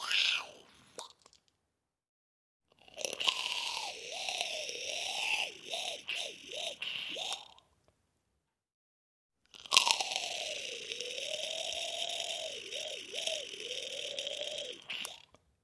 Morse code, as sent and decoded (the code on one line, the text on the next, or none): none